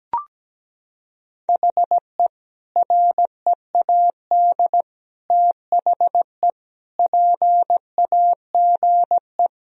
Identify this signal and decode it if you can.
Morse code, heard as a keyed tone, about 17 wpm, 700 Hz. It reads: HE READ THE PAGE
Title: Sentences from Top 1000 Words - Review Part 1 - 17wpm